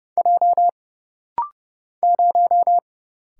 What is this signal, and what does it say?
Morse code, heard as a keyed tone, about 30 wpm, 700 Hz. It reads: J 0
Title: Single Letter-Number - Rapid-Fire 30wpm